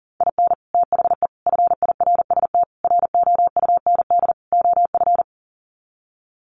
Morse code, heard as a keyed tone, about 40 wpm, 700 Hz. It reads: IN THE FIRST ROUND OF
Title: Sets of 5 Words Encore 40wpm